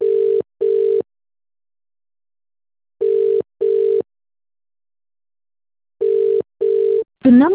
Ringback tone (UK), 400 + 450 Hz.